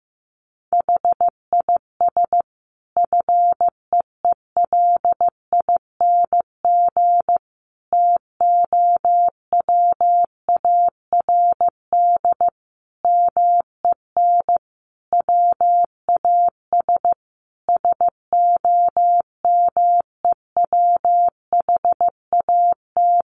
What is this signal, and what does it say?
Morse code, heard as a keyed tone, about 14 wpm, 700 Hz. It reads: HIS FEELING TOWARD MEN WAS SOMEWHAT
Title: Sentences from Top 1300 Words 15wpm